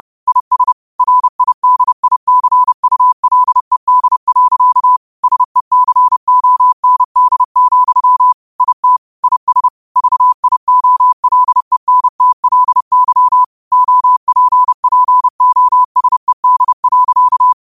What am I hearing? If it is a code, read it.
Morse code, heard as a keyed tone, about 30 wpm, 1000 Hz. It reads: IS RIDICULED. SECOND, IT IS VIOLENTLY OPPOSED.
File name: qotdfiles / 30wpm